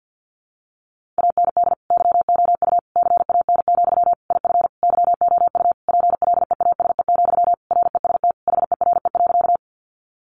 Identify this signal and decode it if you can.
Morse code, heard as a keyed tone, about 50 wpm, 700 Hz. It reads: AND YOU CAN, IF YOU PLEASE, REST HERE.
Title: Sentences from Top 1500 Words 50wpm